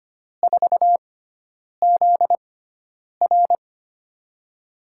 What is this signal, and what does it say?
Morse code, heard as a keyed tone, about 25 wpm, spaced as for 9 wpm, 700 Hz. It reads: 4ZR
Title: USA Call Signs - 5 Characters - 6x Character Spacing - 25wpm